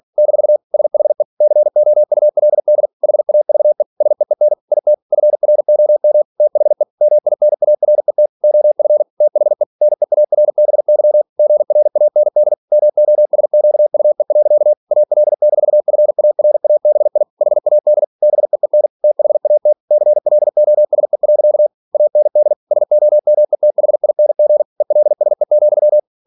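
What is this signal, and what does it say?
Morse code, heard as a keyed tone, about 46 wpm, 600 Hz. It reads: - SHE COULD HAVE SEEN IT FROM THE MINARET OF THE NEARBY GRAND MOSQUE. AL-FARABI HAD BEEN THAT CLOSE. AND SOMETHING ELSE,
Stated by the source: Practice text from an English novel sent at 47 WPM.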